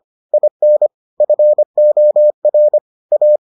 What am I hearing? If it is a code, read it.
Morse code, heard as a keyed tone, about 25 wpm, 600 Hz. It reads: IN FOR A